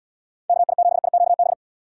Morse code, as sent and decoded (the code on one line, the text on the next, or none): -.- .. --... .. --.- -...
KI7IQB